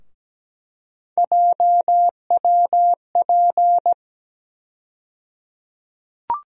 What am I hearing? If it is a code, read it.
Morse code, heard as a keyed tone, about 17 wpm, 700 Hz. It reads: JWP
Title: USA Call Sign Postfixes - 3 Letters 17wpm